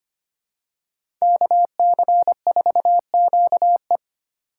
Morse code, heard as a keyed tone, about 25 wpm, 700 Hz. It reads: KC4QE
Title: USA Call Signs - 5 Characters - 6x Character Spacing - 25wpm